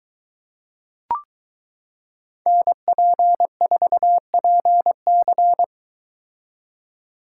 Morse code, heard as a keyed tone, about 22 wpm, 700 Hz. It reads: NP4PC